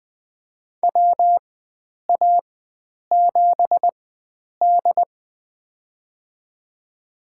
Morse code, heard as a keyed tone, about 20 wpm, spaced as for 9 wpm, 700 Hz. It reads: WA7D
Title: USA Call Signs - 4 Characters - 4x Character Spacing - 20wpm